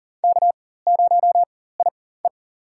Morse code, heard as a keyed tone, about 40 wpm, spaced as for 19 wpm, 700 Hz. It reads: K0IE
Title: USA Call Signs - 4 Characters - 4x Character Spacing - 40wpm